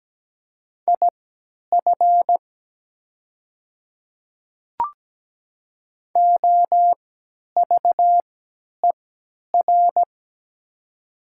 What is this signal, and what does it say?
Morse code, heard as a keyed tone, about 17 wpm, spaced as for 10 wpm, 690 Hz. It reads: IF OVER
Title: 3x Character Spacing - Top 200 Words - 17wpm